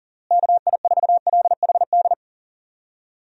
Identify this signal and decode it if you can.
Morse code, heard as a keyed tone, about 40 wpm, 700 Hz. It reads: KI4LHD
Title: USA Call Signs - 6 Characters - Repeated 2x - 40wpm